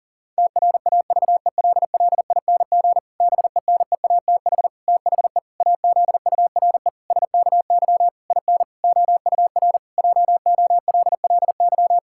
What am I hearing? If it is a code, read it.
Morse code, heard as a keyed tone, about 40 wpm, 700 Hz. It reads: TRAVELLING BENEATH THE AZURE SKY IN OUR JOLLY
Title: Pangram Challenge v2 - 40wpm